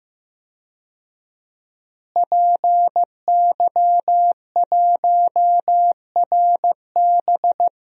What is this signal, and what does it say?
Morse code, heard as a keyed tone, about 15 wpm, 700 Hz. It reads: PY1RB
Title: DXCC-100 Call Signs - 5 Characters - Repeated 2x - 15wpm